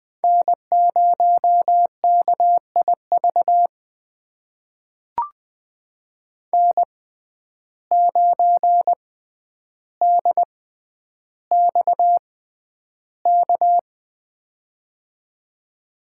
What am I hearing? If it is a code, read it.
Morse code, heard as a keyed tone, about 20 wpm, spaced as for 10 wpm, 700 Hz. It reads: N0KIV N9DXK
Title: USA Call Signs - 5 Characters - 6x Character Spacing - 20wpm